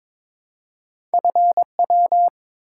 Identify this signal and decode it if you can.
Morse code, heard as a keyed tone, about 23 wpm, 700 Hz. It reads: FW
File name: practice / 22wpm / Two Letter-Number Combinations - 22wpm